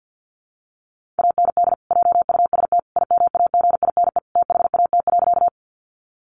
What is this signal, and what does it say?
Morse code, heard as a keyed tone, about 50 wpm, 700 Hz. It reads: AND JUST IMAGINE THAT.
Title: Sentences from Top 1400 Words 50wpm